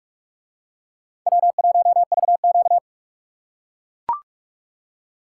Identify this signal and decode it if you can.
Morse code, heard as a keyed tone, about 40 wpm, 700 Hz. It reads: W1VQ